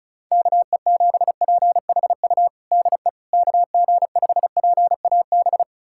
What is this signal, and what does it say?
Morse code, heard as a keyed tone, about 34 wpm, 700 Hz. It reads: KE7PHU DE KG5PAB
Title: USA Callsign DE Callsign - 6 Characters - 35wpm